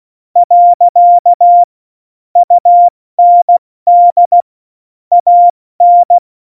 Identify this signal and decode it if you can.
Morse code, heard as a keyed tone, about 16 wpm, 700 Hz. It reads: . UND AN
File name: Morsekurs / Witze / Witz 18. Februar 2020, 70 F80